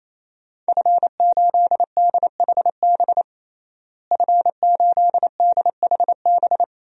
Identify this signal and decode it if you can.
Morse code, heard as a keyed tone, about 28 wpm, 700 Hz. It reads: F8DHB F8DHB
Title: DXCC-100 Call Signs - 5 Characters - Repeated 3x - 28wpm